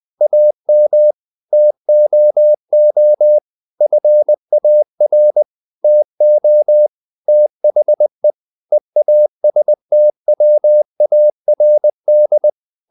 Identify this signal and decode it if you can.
Morse code, heard as a keyed tone, about 20 wpm, 600 Hz. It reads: AM TOO FAR TO THE EASTWARD
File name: the old man and the sea part 2 20wpm